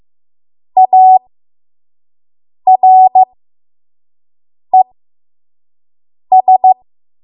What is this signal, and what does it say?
Morse code, heard as a keyed tone, about 15 wpm, spaced as for 5 wpm, 750 Hz. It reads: ARES
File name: Morse / 05WPM